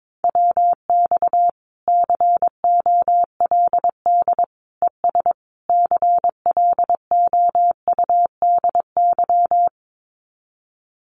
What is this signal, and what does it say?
Morse code, heard as a keyed tone, about 22 wpm, 700 Hz. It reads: WX COLD ES CLOUDY
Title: Common QSO Phrases 22wpm